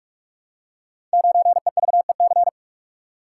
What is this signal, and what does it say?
Morse code, heard as a keyed tone, about 44 wpm, 700 Hz. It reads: 9EVEC
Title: Code Groups - 5 Letter:Numbers - 45wpm